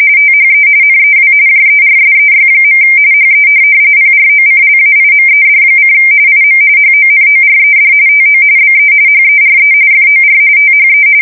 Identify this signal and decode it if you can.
RTTY, about 45.45 baud. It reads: THE NEW JERSEY. HE PROCLAIMED: IM HONORED TO BE HERE FOR THE
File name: fsk170 / USS-New-Jersey